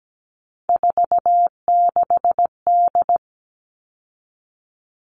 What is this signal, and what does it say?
Morse code, heard as a keyed tone, about 17 wpm, 700 Hz. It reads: 46D
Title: All Cut Numbers - 3-Digit 17wpm